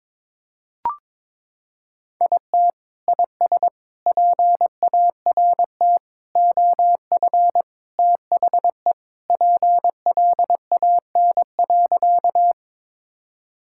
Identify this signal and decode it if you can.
Morse code, heard as a keyed tone, about 22 wpm, 700 Hz. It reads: IT IS PART OF THE PLAN.
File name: practice / 22wpm / Sentences with 4-Word Phrases - 22wpm